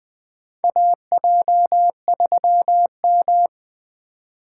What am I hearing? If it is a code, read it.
Morse code, heard as a keyed tone, about 20 wpm, 700 Hz. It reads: AJ3M
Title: USA Call Signs - 4 Characters - 2x Character Spacing - 20wpm